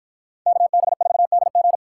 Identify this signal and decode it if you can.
Morse code, heard as a keyed tone, about 53 wpm, 700 Hz. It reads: KB4DG